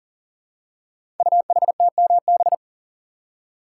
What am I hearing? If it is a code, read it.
Morse code, heard as a keyed tone, about 42 wpm, 690 Hz. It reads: UHTMB
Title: Code Groups - 5 Letter:Numbers - 40wpm